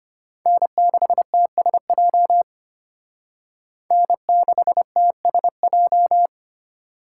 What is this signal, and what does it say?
Morse code, heard as a keyed tone, about 33 wpm, 700 Hz. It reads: N6TSJ N6TSJ